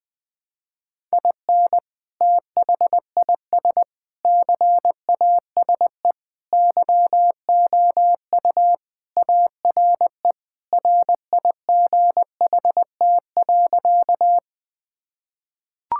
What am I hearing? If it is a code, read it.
Morse code, heard as a keyed tone, about 21 wpm, 700 Hz. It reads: IN THIS CASE YOU ARE RIGHT.
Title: Sentences with 3-Word Phrases - 20wpm